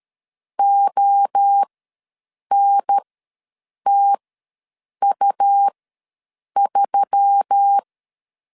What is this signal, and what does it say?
Morse code, heard as a keyed tone, about 13 wpm, spaced as for 7 wpm, 800 Hz. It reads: ONTU3
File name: LEZIONI / 6 PKX0 / 6 P ws 12wpm 3117